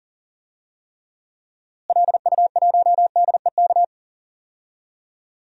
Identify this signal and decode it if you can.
Morse code, heard as a keyed tone, about 40 wpm, 700 Hz. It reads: LU1DEK
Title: DXCC-100 Call Signs - 6 Characters - Repeated 3x - 40wpm